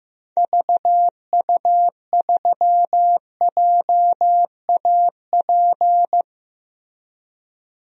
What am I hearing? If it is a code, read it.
Morse code, heard as a keyed tone, about 16 wpm, 700 Hz. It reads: VU3JAP
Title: DXCC-100 Call Signs - 6 Characters - Repeated 3x - 15wpm